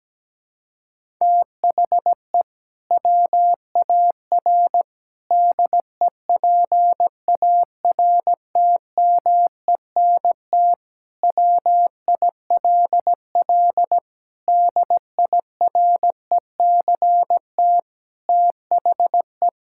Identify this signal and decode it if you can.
Morse code, heard as a keyed tone, about 17 wpm, 700 Hz. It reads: THE WAR DEPARTMENT WILL DIRECT THE
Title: Sentences from Top 1000 Words - Review Part 5 - 17wpm